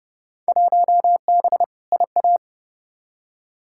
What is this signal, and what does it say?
Morse code, heard as a keyed tone, about 30 wpm, 700 Hz. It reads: 1B IA
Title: Field Day Exchanges Encore - 30wpm